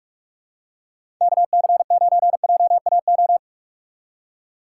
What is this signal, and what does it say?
Morse code, heard as a keyed tone, about 44 wpm, 700 Hz. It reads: KC9JAO
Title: USA Call Signs - 6 Characters - 2x Character Spacing - 45wpm